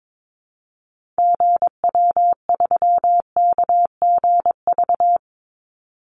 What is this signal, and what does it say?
Morse code, heard as a keyed tone, about 22 wpm, 700 Hz. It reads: GW3KGV
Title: Call Signs Repeated 3x - 22wpm